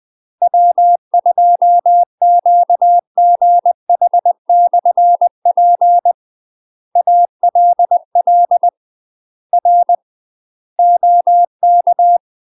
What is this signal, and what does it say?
Morse code, heard as a keyed tone, about 20 wpm, 700 Hz. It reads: W2QGH/P ALL R OK
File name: UK MASTER QSOs TRAINING / 20WPM / UK MASTER QSOs 0016